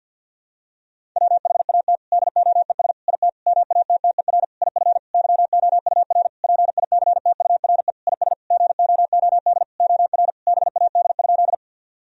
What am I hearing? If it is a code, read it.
Morse code, heard as a keyed tone, about 50 wpm, 700 Hz. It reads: WHAT DOES IT MATTER IF YOUR PICTURE IS GOOD OR BAD?
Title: Sentences from Top 1000 Words - Review Part 1 - 50wpm